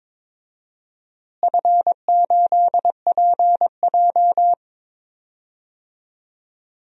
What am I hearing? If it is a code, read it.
Morse code, heard as a keyed tone, about 22 wpm, 700 Hz. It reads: F8PJ
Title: DXCC-100 Call Signs - 4 Characters - Repeated 2x - 22wpm